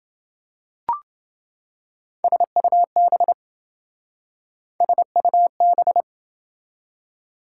Morse code, heard as a keyed tone, about 28 wpm, 700 Hz. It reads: SUB SUB